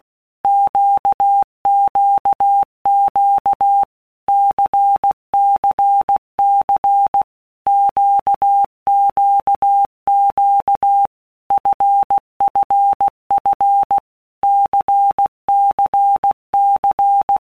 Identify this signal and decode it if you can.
Morse code, heard as a keyed tone, about 16 wpm, 800 Hz. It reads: QQQ CCC QQQ FFF CCC
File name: Basic Letter Groups By Blocks / Groups@15 / LFCYQ@15